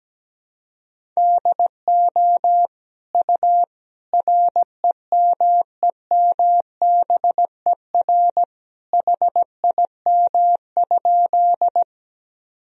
Morse code, heard as a keyed tone, about 17 wpm, 700 Hz. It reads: DO U REMEMBER HIM?
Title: Sentences from Top 400 Words and Abbreviations 17wpm